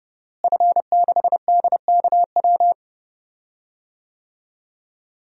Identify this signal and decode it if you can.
Morse code, heard as a keyed tone, about 30 wpm, 700 Hz. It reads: F6DKW
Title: DXCC-100 Call Signs - 5 Characters - Repeated 3x - 30wpm